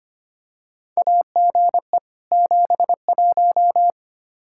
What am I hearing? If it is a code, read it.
Morse code, heard as a keyed tone, about 25 wpm, 700 Hz. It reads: AGE 71
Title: Practice QSO - Exchange 3 - More Info - 25wpm